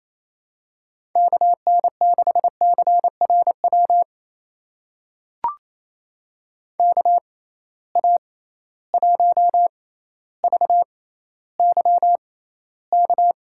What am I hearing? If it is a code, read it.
Morse code, heard as a keyed tone, about 28 wpm, spaced as for 14 wpm, 700 Hz. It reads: KN6CRW KA1VYK
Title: USA Call Signs - 6 Characters - 6x Character Spacing - 28wpm